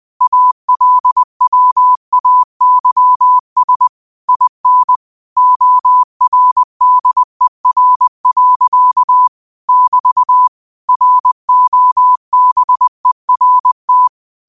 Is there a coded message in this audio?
Yes: Morse code from 0.2 to 14.1 s, heard as a keyed tone, about 20 wpm, 1000 Hz. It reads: ALWAYS IN ORDER. <BT> ROBERT